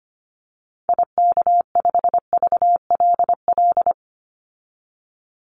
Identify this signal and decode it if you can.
Morse code, heard as a keyed tone, about 25 wpm, 700 Hz. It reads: IK5VLL
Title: Call Signs Repeated 2x - 25wpm